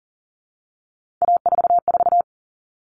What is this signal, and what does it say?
Morse code, heard as a keyed tone, about 40 wpm, 700 Hz. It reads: A44
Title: All Cut Numbers - 3-Digit 40wpm